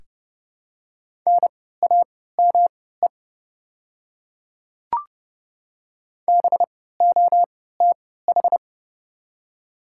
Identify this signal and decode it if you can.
Morse code, heard as a keyed tone, about 30 wpm, spaced as for 17 wpm, 700 Hz. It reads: NAME BOTH
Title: 3x Character Spacing - Top 300 Words - 30wpm